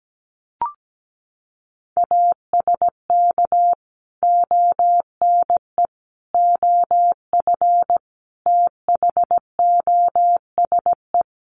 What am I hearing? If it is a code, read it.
Morse code, heard as a keyed tone, about 17 wpm, 700 Hz. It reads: ASK ONE OF THOSE
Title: Sentences from Top 300 Words 17wpm